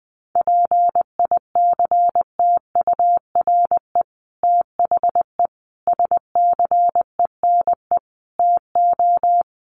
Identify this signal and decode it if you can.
Morse code, heard as a keyed tone, about 20 wpm, 700 Hz. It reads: PICTURE THE SCENE TO
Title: Sentences from Top 1100 Words 20wpm